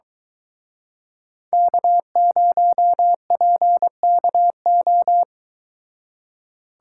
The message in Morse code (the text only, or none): K0PKO